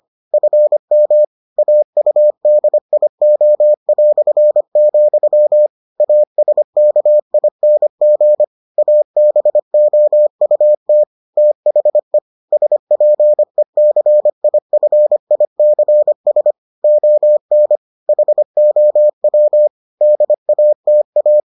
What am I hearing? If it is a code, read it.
Morse code, heard as a keyed tone, about 25 wpm, 600 Hz. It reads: FM AUDIO", ASKING ABOUT THE SPECIFICS ON HOW DATA